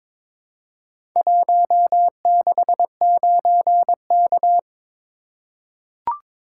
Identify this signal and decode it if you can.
Morse code, heard as a keyed tone, about 23 wpm, 700 Hz. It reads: 169K
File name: practice / 22wpm / Four Letter-Number Combinations - 22wpm